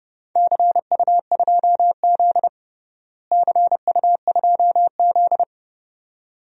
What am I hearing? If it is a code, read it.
Morse code, heard as a keyed tone, about 29 wpm, 700 Hz. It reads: CU2Z CU2Z
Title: DXCC-100 Call Signs - 4 Characters - Repeated 2x - 30wpm